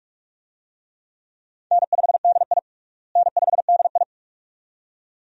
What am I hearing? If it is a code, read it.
Morse code, heard as a keyed tone, about 43 wpm, 700 Hz. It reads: N5DI N5DI
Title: USA Call Signs - 4 Characters - Repeated 2x - 45wpm